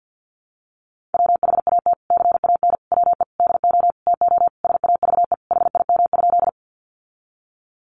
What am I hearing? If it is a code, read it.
Morse code, heard as a keyed tone, about 48 wpm, 700 Hz. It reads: WHAT CAN WE DO TO SAVE HIM?
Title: Sentences from Top 1100 Words 50wpm